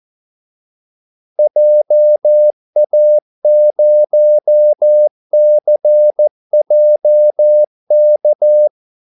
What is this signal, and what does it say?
Morse code, heard as a keyed tone, about 14 wpm, 600 Hz. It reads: JA0CJK
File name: Callsigns-14wpm